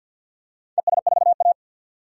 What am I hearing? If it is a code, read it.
Morse code, heard as a keyed tone, about 51 wpm, 700 Hz. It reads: ES4A
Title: DXCC-100 Call Signs - 4 Characters - 50wpm